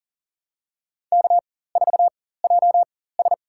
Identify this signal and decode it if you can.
Morse code, heard as a keyed tone, about 40 wpm, spaced as for 19 wpm, 700 Hz. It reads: K4JS
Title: USA Call Signs - 4 Characters - 4x Character Spacing - 40wpm